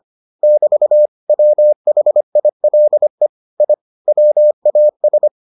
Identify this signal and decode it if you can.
Morse code, heard as a keyed tone, about 25 wpm, 600 Hz. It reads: <BT> WHILE I WAS